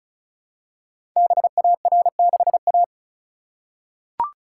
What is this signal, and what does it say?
Morse code, heard as a keyed tone, about 36 wpm, 700 Hz. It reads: BAR6A
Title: Code Groups - 5 Letter:Numbers - 35wpm